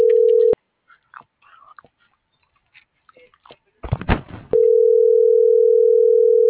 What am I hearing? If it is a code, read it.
Ringback tone (North America), 440 + 480 Hz.